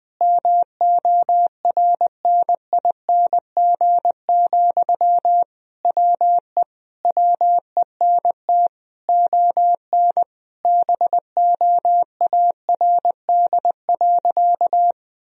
Morse code, heard as a keyed tone, about 20 wpm, 700 Hz. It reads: MORNING, WE WENT ON BOARD.
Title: Sentences from Top 1000 Words - Review Part 4 - 20wpm